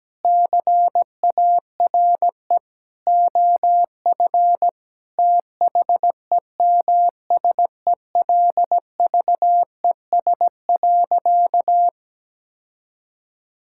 Morse code, heard as a keyed tone, about 17 wpm, 700 Hz. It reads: CARE OF THEMSELVES.
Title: Sentences from Top 1000 Words - Review Part 5 - 17wpm